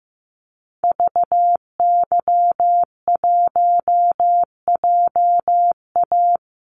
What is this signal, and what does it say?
Morse code, heard as a keyed tone, about 15 wpm, 700 Hz. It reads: VY1JA
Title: Call Signs Repeated 3x - 15wpm